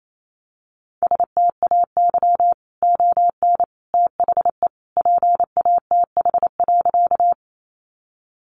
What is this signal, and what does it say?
Morse code, heard as a keyed tone, about 28 wpm, 700 Hz. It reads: STAY ON THE PATH.
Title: A Sound Of Thunder 28wpm